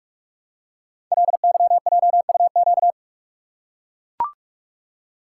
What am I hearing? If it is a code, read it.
Morse code, heard as a keyed tone, about 44 wpm, 700 Hz. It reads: LYJUQ
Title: Code Groups - 5 Letters - 45wpm